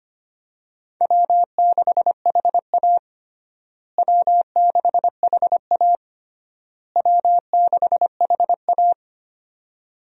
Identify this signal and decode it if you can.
Morse code, heard as a keyed tone, about 25 wpm, 700 Hz. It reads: W6HA W6HA W6HA
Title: USA Call Signs - 4 Characters - Repeated 3x - 25wpm